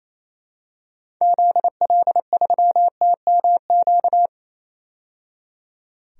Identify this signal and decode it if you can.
Morse code, heard as a keyed tone, about 28 wpm, 700 Hz. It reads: ZL3TMQ
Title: DXCC-100 Call Signs - 6 Characters - 28wpm